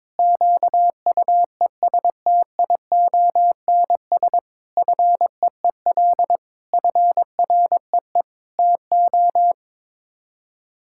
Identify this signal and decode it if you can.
Morse code, heard as a keyed tone, about 22 wpm, 700 Hz. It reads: QUESTIONS FEEL FREE TO
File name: practice / 22wpm / Sets of 8 Words - 22wpm